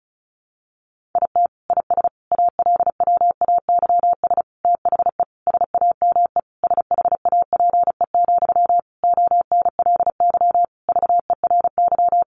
Morse code, heard as a keyed tone, about 34 wpm, 700 Hz. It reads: IT IS ALWAYS THE SAME SHAPE, ONLY VERY
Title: Sentences from Top 1400 Words 35wpm